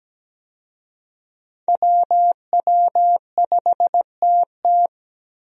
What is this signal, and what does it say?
Morse code, heard as a keyed tone, about 17 wpm, 700 Hz. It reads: WW5TT
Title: USA Call Signs - 5 Characters - Repeated 3x - 17wpm